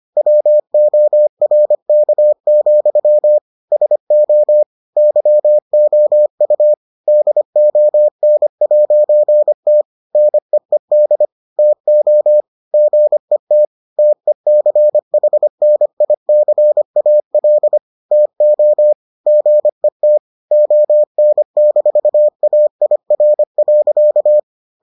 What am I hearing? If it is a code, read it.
Morse code, heard as a keyed tone, about 25 wpm, 600 Hz. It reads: WORK, SO YOU DON'T NEED TO GET TECHNICAL TO GET ON-AIR.